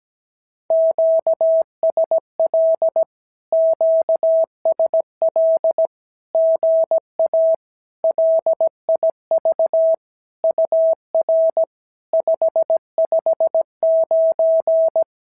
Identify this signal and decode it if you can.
Morse code, heard as a keyed tone, about 17 wpm, 650 Hz. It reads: QSL QSL GA LIV UR 559
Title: SOTA QSO Practice 17wpm